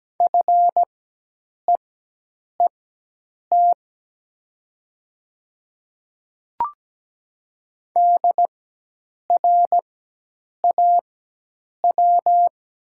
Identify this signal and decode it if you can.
Morse code, heard as a keyed tone, about 17 wpm, spaced as for 8 wpm, 700 Hz. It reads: FEET DRAW
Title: 4x Character Spacing - Top 300 Words - 17wpm